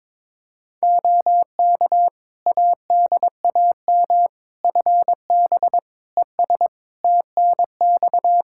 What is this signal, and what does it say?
Morse code, heard as a keyed tone, about 22 wpm, 700 Hz. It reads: OK ADAM FB ES TNX
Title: Practice QSO - Exchange 2 - Info - 22wpm